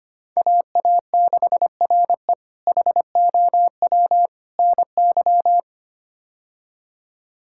Morse code, heard as a keyed tone, about 25 wpm, 700 Hz. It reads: AA6RE HOW NY
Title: K1USN SST QSO Practice - 25wpm